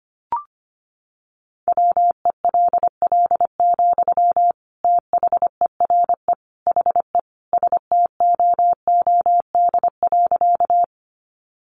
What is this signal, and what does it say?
Morse code, heard as a keyed tone, about 25 wpm, 700 Hz. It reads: WELL, THERE HE STOOD.
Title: Sentences from Top 500 Words 25wpm